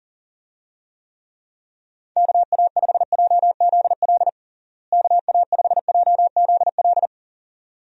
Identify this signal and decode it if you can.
Morse code, heard as a keyed tone, about 40 wpm, 700 Hz. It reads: KA5JZL KA5JZL
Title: USA Call Signs - 6 Characters - Repeated 2x - 40wpm